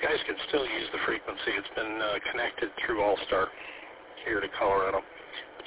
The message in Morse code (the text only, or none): none